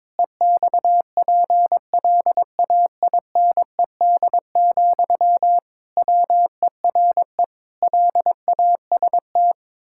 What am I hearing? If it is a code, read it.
Morse code, heard as a keyed tone, about 22 wpm, 700 Hz. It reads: EXPLAINED, WERE LAST
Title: Pangram Challenge v2 - 22wpm